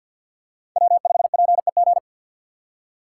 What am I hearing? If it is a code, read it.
Morse code, heard as a keyed tone, about 53 wpm, 700 Hz. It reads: W5PEG
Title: USA Call Signs - 3x Character Spacing - 50wpm